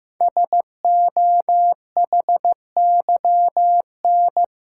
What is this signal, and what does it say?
Morse code, heard as a keyed tone, about 15 wpm, 700 Hz. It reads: SOHYN
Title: Code Groups - 5 Letters - 15wpm